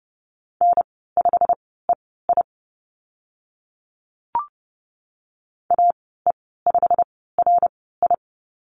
Morse code, heard as a keyed tone, about 30 wpm, spaced as for 17 wpm, 700 Hz. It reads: N5EI AE5RI
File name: Lesson 15g - Number 5 - Call Signs 30wpm